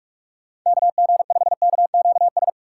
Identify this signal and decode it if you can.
Morse code, heard as a keyed tone, about 45 wpm, 690 Hz. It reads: KG5KQS
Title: USA Call Signs - 6 Characters - 2x Character Spacing - 45wpm